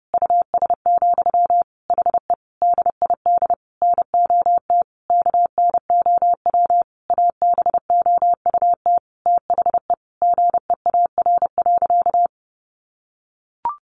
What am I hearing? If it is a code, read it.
Morse code, heard as a keyed tone, about 30 wpm, 700 Hz. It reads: US, HE DID NOT KNOW ABOUT THE GEAR.
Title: Sentences from Top 1400 Words 30wpm